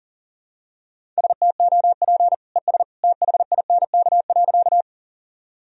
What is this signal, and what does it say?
Morse code, heard as a keyed tone, about 40 wpm, 700 Hz. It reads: STOP ES THINK.
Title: Sentences from Top 300 Words and Abbreviations 40wpm